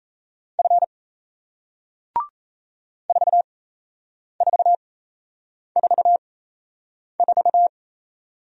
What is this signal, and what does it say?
Morse code, heard as a keyed tone, about 38 wpm, spaced as for 8 wpm, 700 Hz. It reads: F 4444